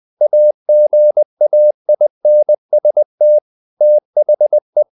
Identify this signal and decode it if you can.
Morse code, heard as a keyed tone, about 20 wpm, 600 Hz. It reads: AGAINST THE
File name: the old man and the sea part 2 20wpm